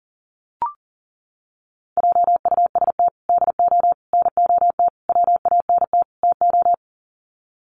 Morse code, heard as a keyed tone, about 39 wpm, 700 Hz. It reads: JUST DO NOT WANT TO
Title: Sets of 5 Words Encore 40wpm